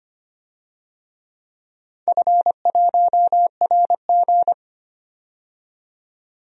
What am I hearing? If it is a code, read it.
Morse code, heard as a keyed tone, about 25 wpm, 700 Hz. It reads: F1RG
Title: DXCC-100 Call Signs - 4 Characters - 25wpm